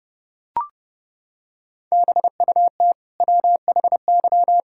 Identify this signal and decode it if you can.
Morse code, heard as a keyed tone, about 30 wpm, 700 Hz. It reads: BUT WHY
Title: Sentences from Top 1000 Words - Review Part 6 - 30wpm